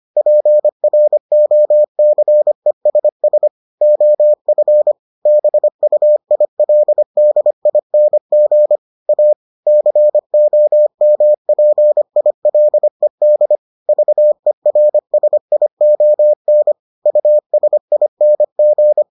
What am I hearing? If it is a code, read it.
Morse code, heard as a keyed tone, about 25 wpm, 600 Hz. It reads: PROCESS OF BUILDING A COMPILED VERSION USING